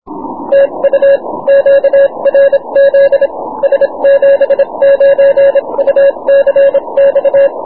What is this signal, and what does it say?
Morse code, heard as a keyed tone, about 27 wpm, spaced as for 23 wpm, 550 Hz. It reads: TU QRZ S79UCX
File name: wp / s79ucx 20m cw